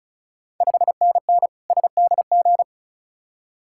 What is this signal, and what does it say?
Morse code, heard as a keyed tone, about 36 wpm, 690 Hz. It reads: 5NN SDG
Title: Ontario QSO Party Exchange 35wpm